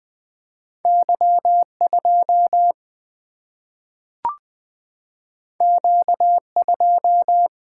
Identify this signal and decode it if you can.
Morse code, heard as a keyed tone, about 20 wpm, 700 Hz. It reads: Y 2 Q 2
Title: Single-Letter Single-Number 20wpm